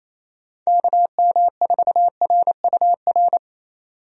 Morse code, heard as a keyed tone, about 27 wpm, 700 Hz. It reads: KM4RUR
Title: USA Call Signs - 6 Characters - 28wpm